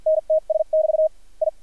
Morse code, heard as a keyed tone, about 53 wpm, spaced as for 43 wpm, 630 Hz, with radio noise. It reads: NTS<BT> I
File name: archived / CRSnet